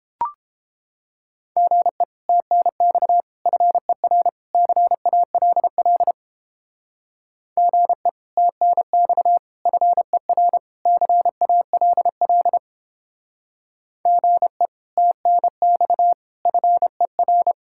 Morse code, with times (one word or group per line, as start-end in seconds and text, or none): 1.6-2.0 GE
2.3-3.2 TNX
3.5-4.3 FER
4.5-6.1 CALL
7.6-8.1 GE
8.4-9.4 TNX
9.7-10.6 FER
10.9-12.6 CALL
14.1-14.7 GE
15.0-16.1 TNX
16.5-17.5 FER